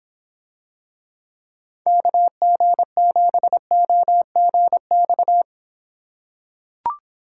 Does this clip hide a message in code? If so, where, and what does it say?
Yes: Morse code from 1.9 to 5.4 s, heard as a keyed tone, about 26 wpm, 700 Hz. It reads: KG7OGX